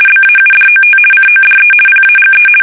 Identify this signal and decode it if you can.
RTTY, about 50 baud. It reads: BAUDOT CODE PAGE
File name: fsk850-50B / teletype-corp